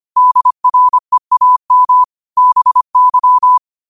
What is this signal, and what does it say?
Morse code, heard as a keyed tone, about 25 wpm, 1000 Hz. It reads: DREAM BY
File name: qotdfiles / 25wpm